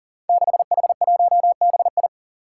Morse code, heard as a keyed tone, about 40 wpm, 700 Hz. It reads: 6H1BI
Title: Five Challenging Letter-Number Combinations 40wpm